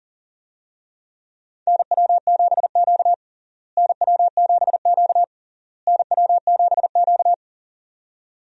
Morse code, heard as a keyed tone, about 40 wpm, 700 Hz. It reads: NW7Q NW7Q NW7Q
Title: USA Call Signs - 4 Characters - Repeated 3x - 40wpm